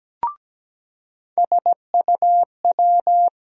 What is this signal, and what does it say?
Morse code, heard as a keyed tone, about 17 wpm, 700 Hz. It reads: SUW
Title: USA Call Sign Postfixes - 3 Letters 17wpm